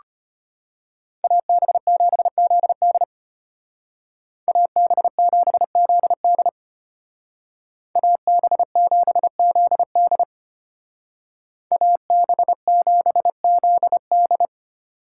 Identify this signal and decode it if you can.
Morse code, heard as a keyed tone, about 30 wpm, 700 Hz. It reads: AB7ZD AB7ZD AB7ZD AB7ZD
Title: Speed Racing - USA 2x2 Call Signs 25wpm